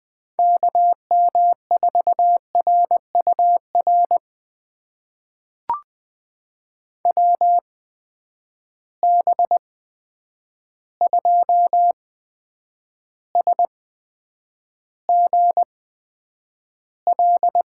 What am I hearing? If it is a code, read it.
Morse code, heard as a keyed tone, about 20 wpm, spaced as for 9 wpm, 700 Hz. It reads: KM4RUR WB2SGL
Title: USA Call Signs - 6 Characters - 8x Character Spacing - 20wpm